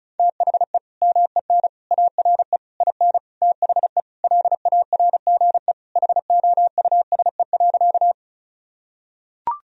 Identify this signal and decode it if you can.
Morse code, heard as a keyed tone, about 36 wpm, 700 Hz. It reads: THE MEN ARE IN THE LARGE HOUSE.
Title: Sentences from Top 1000 Words - Review Part 1 - 35wpm